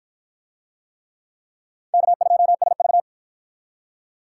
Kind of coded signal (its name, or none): Morse code